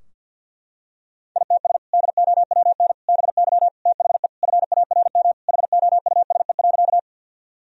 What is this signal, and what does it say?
Morse code, heard as a keyed tone, about 52 wpm, 700 Hz. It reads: ITS DOWN BY THE FARM HOUSE.
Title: Sentences from Top 1000 Words - Review Part 2 - 50wpm